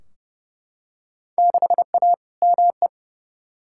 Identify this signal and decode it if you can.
Morse code, heard as a keyed tone, about 30 wpm, 700 Hz. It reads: 6A ME
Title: Field Day Exchanges - 30wpm